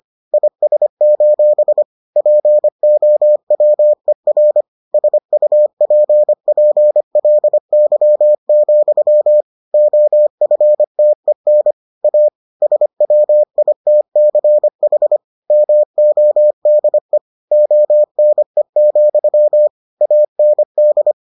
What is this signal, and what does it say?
Morse code, heard as a keyed tone, about 25 wpm, 600 Hz. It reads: IS: POWER SUPPLY, OFTEN A SWITCH MODE ONE, AND